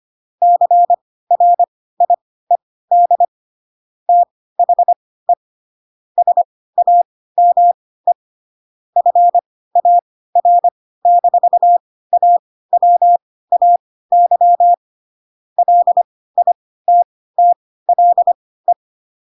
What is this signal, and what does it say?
Morse code, heard as a keyed tone, about 25 wpm, spaced as for 16 wpm, 700 Hz. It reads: CRIED THE SAME FAR-AWAY LITTLE
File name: Pinocchio / Pinocchio ch1 10 25 ik1hge